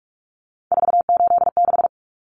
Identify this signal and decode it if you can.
Morse code, heard as a keyed tone, about 44 wpm, 700 Hz. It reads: 486
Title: Common Cut Numbers - 3-Digit 45wpm